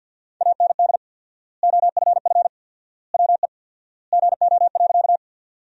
Morse code, heard as a keyed tone, about 47 wpm, 700 Hz. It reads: AND OFF WE GO.
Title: 4x Word-Spacing - Sentences from Top 200 Words - 50wpm